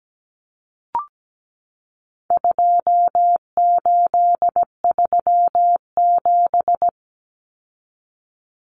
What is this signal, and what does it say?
Morse code, heard as a keyed tone, about 17 wpm, 700 Hz. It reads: 2837
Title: Common Cut Numbers - 4-Digit 17wpm